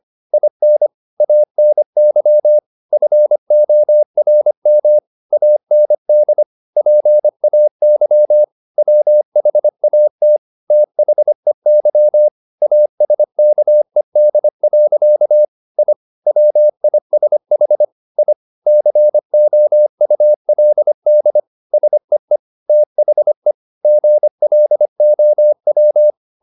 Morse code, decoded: IN ANY FORM AND PAY WHAT THEY ASKED. I WISH I COULD SEE THE GLOW

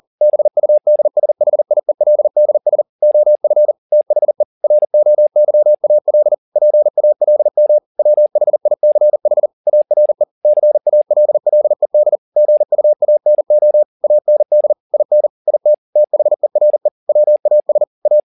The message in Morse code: BUDSHIELDS OF THE ROYAL PALM WHICH ARE CALLED GUANO AND IN IT THERE WAS A

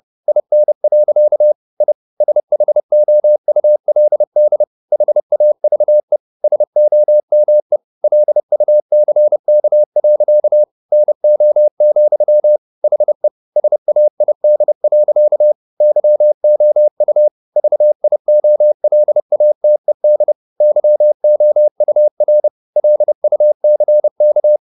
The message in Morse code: IN. I SHOULD HAVE SOME LUCK. NO, HE SAID. YOU VIOLATED YOUR LUCK

